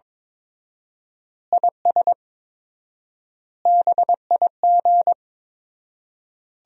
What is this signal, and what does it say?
Morse code, heard as a keyed tone, about 22 wpm, 700 Hz. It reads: IS BIG
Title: 4x Word-Spacing - Sentences from Top 400 Words - 22wpm